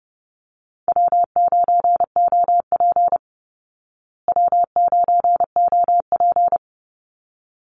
Morse code, heard as a keyed tone, about 30 wpm, 700 Hz. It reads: W9OP W9OP
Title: Call Signs Repeated 3x - 30wpm